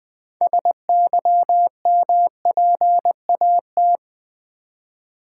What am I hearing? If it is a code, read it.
Morse code, heard as a keyed tone, about 20 wpm, 700 Hz. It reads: SYMPAT
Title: Build A Word - 2 Letters At A Time - 8-Letter Words - 20wpm